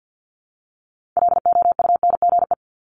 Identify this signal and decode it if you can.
Morse code, heard as a keyed tone, about 50 wpm, 700 Hz. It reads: LOUNGE
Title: Multi-letter Phonemes Repeated 3x 50wpm